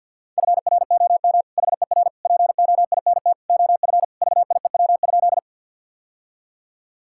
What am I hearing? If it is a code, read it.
Morse code, heard as a keyed tone, about 49 wpm, 700 Hz. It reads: FROM HER POINT OF VIEW?
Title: Sentences from Top 1000 Words - Review Part 6 - 50wpm